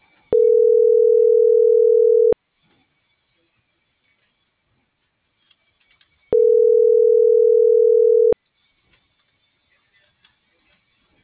Ringback tone (North America), 440 + 480 Hz.